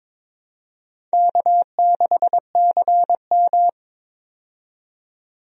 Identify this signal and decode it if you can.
Morse code, heard as a keyed tone, about 22 wpm, 700 Hz. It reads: K6CM
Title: USA Call Signs - 4 Characters - 22wpm